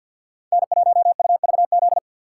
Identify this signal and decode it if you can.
Morse code, heard as a keyed tone, about 53 wpm, 700 Hz. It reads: N1UVZ
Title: USA Call Signs - 5 Characters - 6x Character Spacing - 50wpm